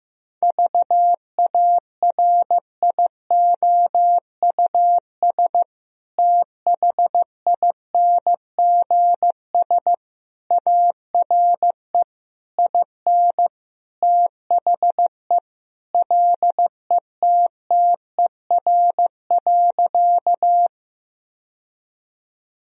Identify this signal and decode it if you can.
Morse code, heard as a keyed tone, about 15 wpm, 700 Hz. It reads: VARIOUS THINGS ARE IN THE LETTER.
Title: Sentences from Top 1000 Words - Review Part 4 - 15wpm